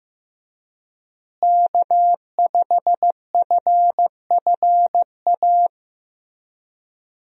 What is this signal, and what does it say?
Morse code, heard as a keyed tone, about 15 wpm, 700 Hz. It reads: K5FFA
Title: USA Call Signs - 5 Characters - Repeated 3x - 15wpm